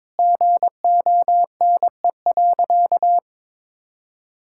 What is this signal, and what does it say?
Morse code, heard as a keyed tone, about 22 wpm, 700 Hz. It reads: GONE.
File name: practice / 22wpm / Sentences with 4-Word Phrases - 22wpm